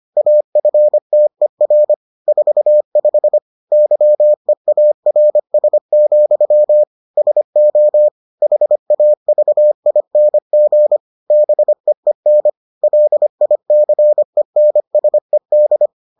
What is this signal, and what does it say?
Morse code, heard as a keyed tone, about 25 wpm, 600 Hz. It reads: AFTER 45 YEARS, SO HAVING BEEN LICENSED